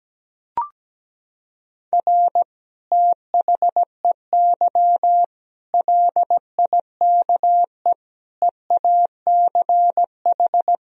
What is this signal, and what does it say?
Morse code, heard as a keyed tone, about 17 wpm, 700 Hz. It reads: R THEY LIKE EACH
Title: Sentences from Top 200 Words and Abbreviations 17wpm